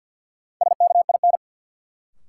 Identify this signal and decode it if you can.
Morse code, heard as a keyed tone, about 51 wpm, 700 Hz. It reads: SKIN
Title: Sentences from Top 1300 Words - New Word Drill Alt - 50wpm